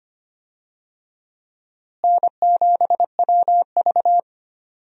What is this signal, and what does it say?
Morse code, heard as a keyed tone, about 25 wpm, 700 Hz. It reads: N7WV
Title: USA Call Signs - 4 Characters - Repeated 2x - 25wpm